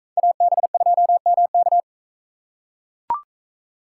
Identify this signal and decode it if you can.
Morse code, heard as a keyed tone, about 41 wpm, 700 Hz. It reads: AB2MK